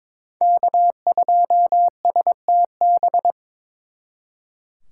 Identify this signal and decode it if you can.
Morse code, heard as a keyed tone, about 26 wpm, 700 Hz. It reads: K2STB
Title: Speed Racing - USA 1x3 Call Signs 22wpm